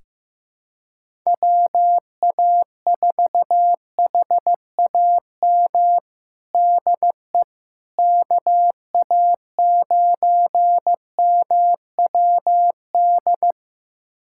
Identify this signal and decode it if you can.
Morse code, heard as a keyed tone, about 16 wpm, 700 Hz. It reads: WA4HAM DE KA9MWD
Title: USA Callsign DE Callsign - 6 Characters - 15wpm